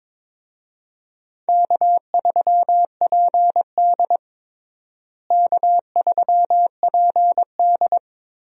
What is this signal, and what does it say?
Morse code, heard as a keyed tone, about 21 wpm, 700 Hz. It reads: K3PD K3PD
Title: USA Call Signs - 4 Characters - Repeated 3x - 22wpm